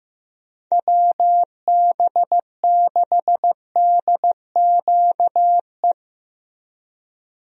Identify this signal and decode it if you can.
Morse code, heard as a keyed tone, about 15 wpm, 700 Hz. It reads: WB6DQE
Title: USA Call Signs - 6 Characters - Repeated 2x - 15wpm